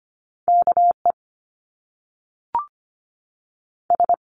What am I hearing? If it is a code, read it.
Morse code, heard as a keyed tone, about 25 wpm, 700 Hz. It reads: K E S
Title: Top Two Letter Combinations in Words 25wpm